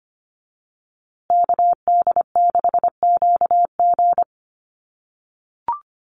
Morse code, heard as a keyed tone, about 25 wpm, 700 Hz. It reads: KD6QG
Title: USA Call Signs Repeated 3x 25wpm